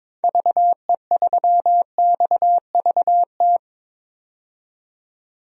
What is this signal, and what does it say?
Morse code, heard as a keyed tone, about 22 wpm, 700 Hz. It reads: VE3XVT
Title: DXCC-100 Call Signs - 6 Characters - 22wpm